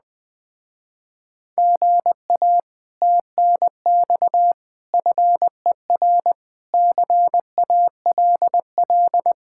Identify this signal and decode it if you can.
Morse code, heard as a keyed tone, about 20 wpm, 700 Hz. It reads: GA TNX FER CALL
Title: Speed-Racing Common QSO Phrases - 17wpm